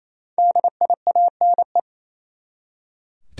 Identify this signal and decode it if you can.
Morse code, heard as a keyed tone, about 28 wpm, 700 Hz. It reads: DIANE
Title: USA Common Names - Max 5 Letters - 28wpm